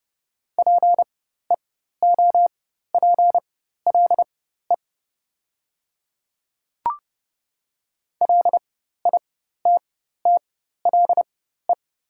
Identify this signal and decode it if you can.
Morse code, heard as a keyed tone, about 30 wpm, spaced as for 14 wpm, 700 Hz. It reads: PEOPLE LITTLE
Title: 4x Character Spacing - Top 200 Words - 30wpm